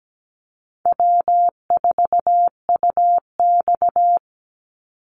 Morse code, heard as a keyed tone, about 17 wpm, 700 Hz. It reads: W4UX
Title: Easy Call Signs - 17wpm